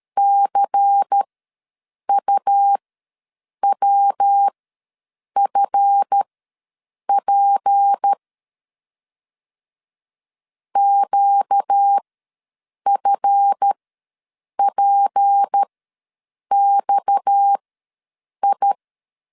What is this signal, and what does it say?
Morse code, heard as a keyed tone, about 13 wpm, spaced as for 7 wpm, 800 Hz. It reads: CUWFP QFPXI
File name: LEZIONI / 7 GF9Z / 7 GF ws 12wpm 3140